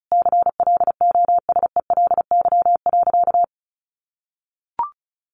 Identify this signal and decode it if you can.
Morse code, heard as a keyed tone, about 34 wpm, 700 Hz. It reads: CLOSELY.
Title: Sentences from Top 1400 Words 35wpm